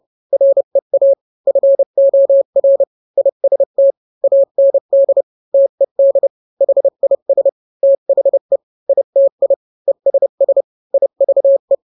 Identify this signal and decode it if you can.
Morse code, heard as a keyed tone, about 30 wpm, 550 Hz. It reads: REA FOR IST AND TED HIS THE ITI ESS IVE